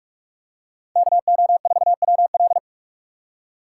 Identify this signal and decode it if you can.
Morse code, heard as a keyed tone, about 46 wpm, 700 Hz. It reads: KO4WL
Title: USA Call Signs - 5 Characters - 4x Character Spacing - 45wpm